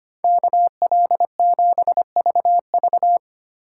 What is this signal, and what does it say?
Morse code, heard as a keyed tone, about 25 wpm, 700 Hz. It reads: KL7VV
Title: USA Call Signs - 5 Characters - 3x Character Spacing - 25wpm